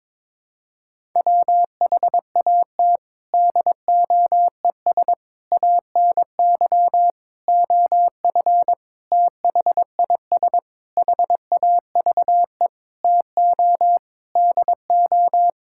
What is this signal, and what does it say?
Morse code, heard as a keyed tone, about 22 wpm, 700 Hz. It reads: WHAT DOES ANY OF THIS HAVE TO DO
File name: Sentences with 9-Word Phrases - 22wpm